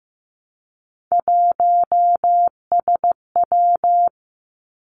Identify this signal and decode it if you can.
Morse code, heard as a keyed tone, about 15 wpm, 700 Hz. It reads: 1SW
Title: Three Challenging Letter-Number Combinations 15wpm